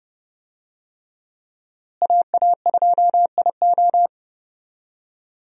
Morse code, heard as a keyed tone, about 30 wpm, 700 Hz. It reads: AA2IO
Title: USA Call Signs - 5 Characters - Repeated 3x - 30wpm